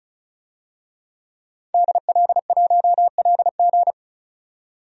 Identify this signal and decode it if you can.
Morse code, heard as a keyed tone, about 36 wpm, 700 Hz. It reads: DL1LG
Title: DXCC-100 Call Signs - 5 Characters - 35wpm